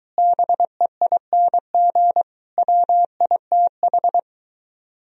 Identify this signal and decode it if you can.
Morse code, heard as a keyed tone, about 23 wpm, 700 Hz. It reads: BEING WITH